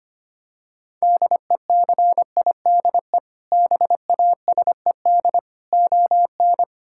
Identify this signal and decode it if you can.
Morse code, heard as a keyed tone, about 25 wpm, 700 Hz. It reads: DECIDE BASED ON
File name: practice / 25wpm / Sentences with 2-Word Phrases - 25wpm